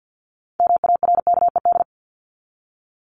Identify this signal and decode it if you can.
Morse code, heard as a keyed tone, about 50 wpm, 700 Hz. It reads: MARKED
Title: ED Suffix - Top 100 Words with Base Words 50wpm